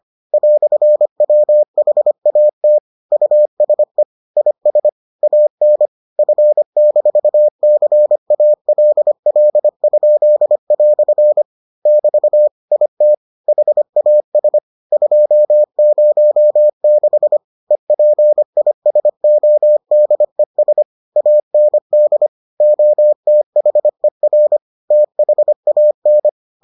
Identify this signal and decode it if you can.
Morse code, heard as a keyed tone, about 25 wpm, 600 Hz. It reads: "WHAT USE IS AN F-CALL?" <BT> IT HAS 206 EPISODES AND OTHER THAN